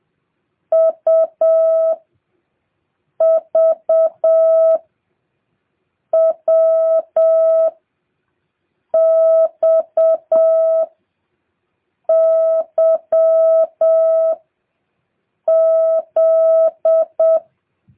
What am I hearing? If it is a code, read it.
Morse code, heard as a keyed tone, about 7 wpm, 660 Hz. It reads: U V W X Y Z